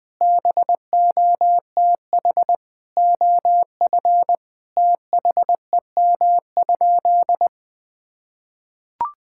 Morse code, heard as a keyed tone, about 20 wpm, 700 Hz. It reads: BOTH OF THEM?
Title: Sentences from Top 1000 Words - Review Part 5 - 20wpm